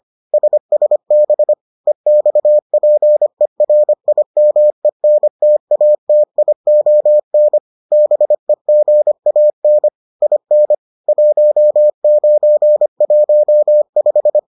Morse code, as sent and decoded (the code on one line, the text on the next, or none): ... ... -... / . -..- .--. .-. .. -- . -. - .- - .. --- -. / -... --. .- -. / .. -. / .---- ----. .---- .....
SSB EXPERIMENTATION BEGAN IN 1915